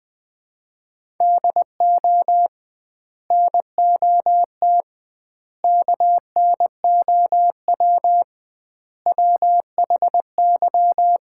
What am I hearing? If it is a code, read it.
Morse code, heard as a keyed tone, about 20 wpm, 700 Hz. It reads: DO NOT KNOW WHY
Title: 2x Word-Spacing - Sets of 4 Words - 20wpm